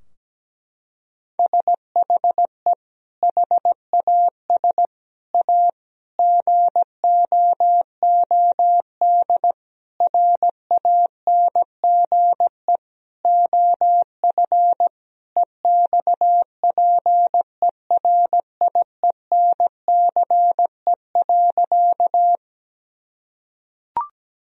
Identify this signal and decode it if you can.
Morse code, heard as a keyed tone, about 17 wpm, 700 Hz. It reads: SHE HAS A GOOD RANGE OF EXPERIENCE.
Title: Sentences from Top 1000 Words - Review Part 4 - 17wpm